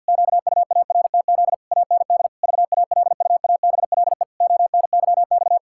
Morse code, heard as a keyed tone, about 50 wpm, 700 Hz. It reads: QUARTZ AND VALUABLE ONYX
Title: Pangram Challenge v2 - 50wpm